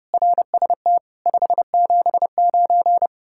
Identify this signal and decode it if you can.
Morse code, heard as a keyed tone, about 38 wpm, 700 Hz. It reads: RST 579
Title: Speed-Racing Common QSO Phrases - 30wpm